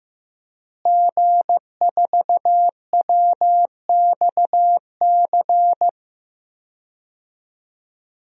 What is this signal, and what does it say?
Morse code, heard as a keyed tone, about 15 wpm, 700 Hz. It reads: G4WXC
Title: DXCC-100 Call Signs - 5 Characters - Repeated 2x - 15wpm